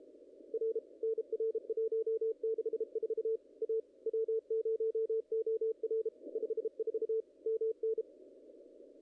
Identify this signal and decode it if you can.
Morse code, heard as a keyed tone, about 33 wpm, 440 Hz, with radio noise. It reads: R NR 164 A W0OR 54 MN
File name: contests / ss-cw-k5tr-op-n5rz -0000